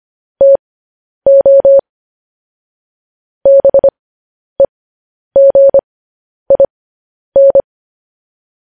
Morse code, heard as a keyed tone, about 25 wpm, spaced as for 10 wpm, 550 Hz. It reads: TO BEGIN
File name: podcast / News headlines-1x-25x10